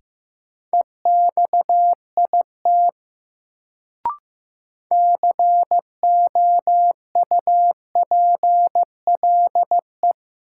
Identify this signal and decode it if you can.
Morse code, heard as a keyed tone, about 15 wpm, 700 Hz. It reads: EXIT COUPLE
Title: Sentences from Top 1100 Words - New Word Drill - 15wpm